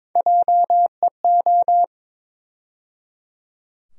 Morse code, heard as a keyed tone, about 22 wpm, 700 Hz. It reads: JEO
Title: USA Call Sign Postfixes - 3 Letters 22wpm